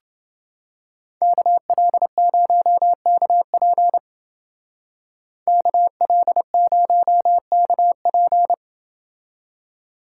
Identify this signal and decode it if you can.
Morse code, heard as a keyed tone, about 28 wpm, 700 Hz. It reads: KL0KP KL0KP